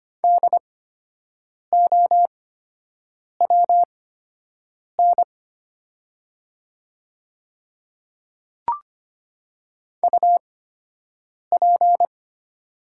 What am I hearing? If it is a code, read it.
Morse code, heard as a keyed tone, about 25 wpm, spaced as for 7 wpm, 700 Hz. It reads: DOWN UP
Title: 8x Character Spacing - Top 100 Words - 25wpm